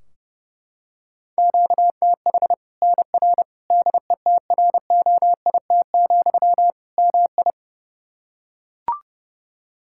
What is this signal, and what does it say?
Morse code, heard as a keyed tone, about 30 wpm, 700 Hz. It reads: QTH NR DETROIT, MI
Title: K1USN SST QSO Practice 30wpm